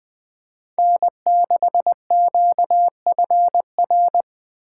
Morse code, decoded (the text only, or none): N6QFR